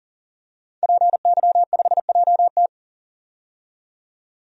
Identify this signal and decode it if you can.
Morse code, heard as a keyed tone, about 40 wpm, 700 Hz. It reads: PY5JT
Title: DXCC-100 Call Signs - 5 Characters - 40wpm